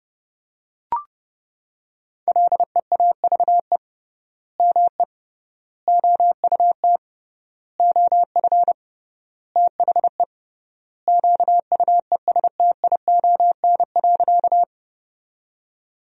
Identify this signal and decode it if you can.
Morse code, heard as a keyed tone, about 32 wpm, 700 Hz. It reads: LEAVE ME OUT OF THE QUESTION.
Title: 3x Word-Spacing - Sentences from Top 400 Words - 30wpm